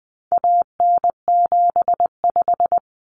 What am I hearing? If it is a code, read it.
Morse code, heard as a keyed tone, about 20 wpm, 700 Hz. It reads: AN75
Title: Common Cut Numbers - 4-Digit 20wpm